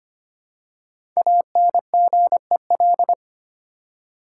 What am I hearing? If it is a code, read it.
Morse code, heard as a keyed tone, about 25 wpm, 700 Hz. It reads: ANGEL
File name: 5 Letter Words - First 3 Letters Given - 25wpm